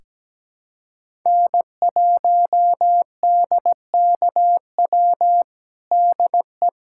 Morse code, heard as a keyed tone, about 17 wpm, 700 Hz. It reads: N1DKW DE
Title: USA Callsign DE Callsign - 5 Characters - 17wpm